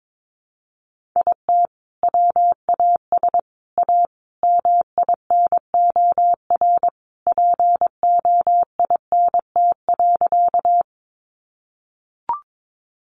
Morse code, heard as a keyed tone, about 22 wpm, 700 Hz. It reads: IT WAS A MINOR POINT.
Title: Sentences from Top 1500 Words 22wpm